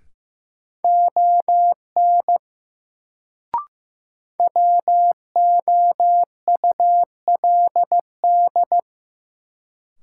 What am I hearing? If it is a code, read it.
Morse code, heard as a keyed tone, about 15 wpm, 700 Hz. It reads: ON WOULD
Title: Sets of 2 Words - New Word Drill - 15wpm